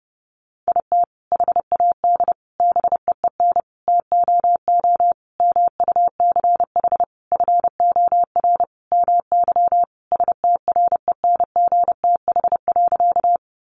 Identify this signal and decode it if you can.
Morse code, heard as a keyed tone, about 30 wpm, 700 Hz. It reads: IT HAD BEEN TOO MUCH FOR MY STRENGTH.
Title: Sentences from Top 1300 Words 30wpm